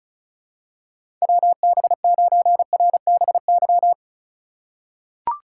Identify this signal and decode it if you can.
Morse code, heard as a keyed tone, about 36 wpm, 700 Hz. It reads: WB9RBY
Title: USA Call Signs - 3x Character Spacing - 35wpm